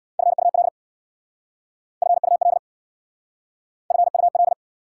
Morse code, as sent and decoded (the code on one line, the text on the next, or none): ..-. ..- .-.. / ..-. ..- .-.. / ..-. ..- .-..
FUL FUL FUL